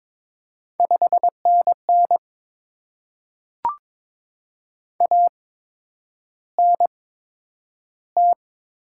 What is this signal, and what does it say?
Morse code, heard as a keyed tone, about 22 wpm, spaced as for 9 wpm, 700 Hz. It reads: 5NN ANT
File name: Top 20 QSO Elements - 8x Character Spacing - 22wpm